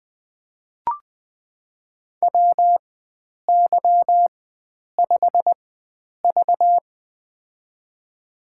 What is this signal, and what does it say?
Morse code, heard as a keyed tone, about 21 wpm, spaced as for 9 wpm, 700 Hz. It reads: WY5V
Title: USA Call Signs - 4 Characters - 4x Character Spacing - 20wpm